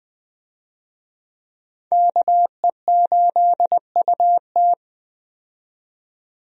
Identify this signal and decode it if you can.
Morse code, heard as a keyed tone, about 20 wpm, 700 Hz. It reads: KE8UT
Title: USA Call Signs - 5 Characters - 20wpm